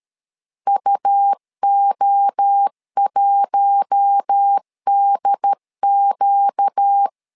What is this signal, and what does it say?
Morse code, heard as a keyed tone, about 13 wpm, 800 Hz. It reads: UO1DQ
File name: LEZIONI / 4 Q1 12wpm 2028